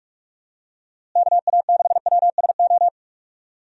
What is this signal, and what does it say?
Morse code, heard as a keyed tone, about 46 wpm, 700 Hz. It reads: KA6WSO
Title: USA Call Signs - 6 Characters - Repeated 2x - 45wpm